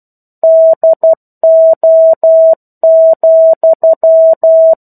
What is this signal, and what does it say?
Morse code, heard as a keyed tone, about 12 wpm, 650 Hz. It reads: DO,
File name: 01-Chapter-01 12 WPM